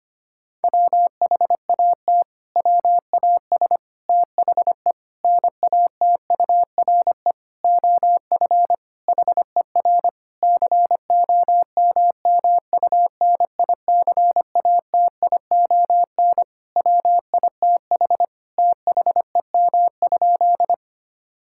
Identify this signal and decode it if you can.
Morse code, heard as a keyed tone, about 25 wpm, 700 Hz. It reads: WHAT WAS THE NATURE OF HER COMMUNICATION WITH THEM?
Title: Sentences from Top 1000 Words - Review Part 6 - 25wpm